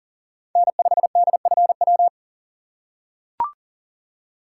Morse code, heard as a keyed tone, about 40 wpm, 700 Hz. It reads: N5DFW
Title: USA Call Signs - 5 Characters - 6x Character Spacing - 40wpm